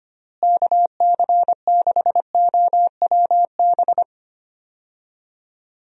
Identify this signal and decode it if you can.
Morse code, heard as a keyed tone, about 25 wpm, 700 Hz. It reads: KC6OWB
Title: USA Call Signs - 6 Characters - Repeated 3x - 25wpm